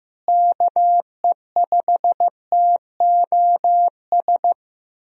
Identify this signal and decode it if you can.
Morse code, heard as a keyed tone, about 15 wpm, 700 Hz. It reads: KE5TOS
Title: USA Call Signs - 6 Characters - Repeated 2x - 15wpm